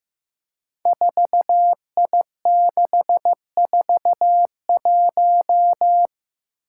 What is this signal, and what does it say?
Morse code, heard as a keyed tone, about 15 wpm, 700 Hz. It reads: 4I641
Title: Five Challenging Letter-Number Combinations 15wpm